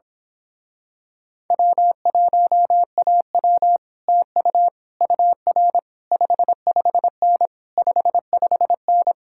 Morse code, heard as a keyed tone, about 26 wpm, 700 Hz. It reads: W1AW TU UR 55N 55N
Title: Speed-Racing Common QSO Phrases Encore - 22wpm